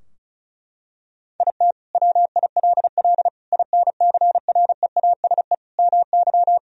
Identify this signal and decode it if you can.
Morse code, heard as a keyed tone, about 36 wpm, 700 Hz. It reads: IT WILL INCREASE MY
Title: Sentences from Top 1000 Words - Review Part 4 - 35wpm